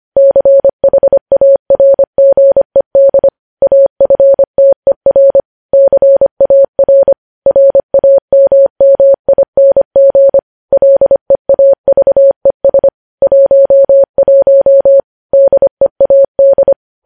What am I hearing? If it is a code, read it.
Morse code, heard as a keyed tone, about 25 wpm, 550 Hz. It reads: CHARGED AFTER CAR RAMMING LEAVES 11 DEAD
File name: podcast / News headlines-1x-25x25